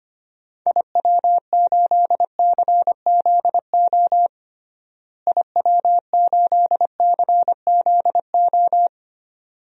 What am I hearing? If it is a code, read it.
Morse code, heard as a keyed tone, about 25 wpm, 700 Hz. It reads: IW8CZO IW8CZO
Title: DXCC-100 Call Signs - 6 Characters - Repeated 2x - 25wpm